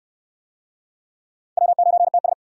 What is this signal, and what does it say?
Morse code, heard as a keyed tone, about 62 wpm, 700 Hz. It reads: W9TS